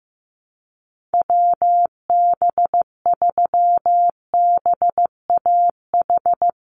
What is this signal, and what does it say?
Morse code, heard as a keyed tone, about 15 wpm, 700 Hz. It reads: WB3BAH
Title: USA Call Signs 15wpm